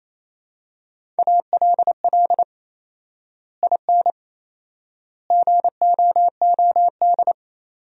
Morse code, heard as a keyed tone, about 27 wpm, 700 Hz. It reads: ALL IN GOOD
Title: 4x Word-Spacing - Sentences from Top 200 Words - 28wpm